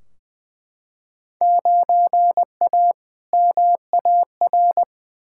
Morse code, heard as a keyed tone, about 20 wpm, 700 Hz. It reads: 9A MAR
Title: Field Day Exchanges Encore - 20wpm